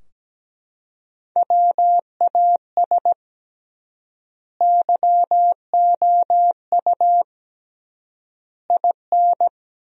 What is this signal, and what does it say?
Morse code, heard as a keyed tone, about 17 wpm, 700 Hz. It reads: WAS YOU IN
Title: 3x Word-Spacing - Sentences from Top 300 Words - 17wpm